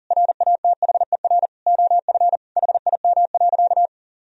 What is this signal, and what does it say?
Morse code, heard as a keyed tone, about 40 wpm, 700 Hz. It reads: RATHER OF HIM.
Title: Sentences from Top 1000 Words - Review Part 4 - 40wpm